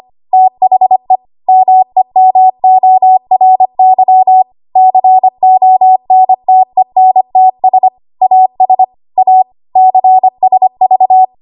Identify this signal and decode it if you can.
Morse code, heard as a keyed tone, about 25 wpm, 750 Hz. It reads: THE MEMORY CONTENTS AS A CSV
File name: Archive / 25 WPM / 25WPM